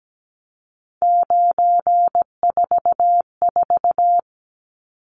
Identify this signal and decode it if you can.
Morse code, heard as a keyed tone, about 17 wpm, 700 Hz. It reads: 944
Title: Numbers - 3-Digit 17wpm